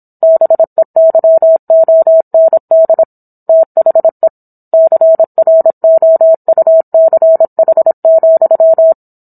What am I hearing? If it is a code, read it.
Morse code, heard as a keyed tone, about 27 wpm, 650 Hz. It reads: BEYOND THE CROUCH,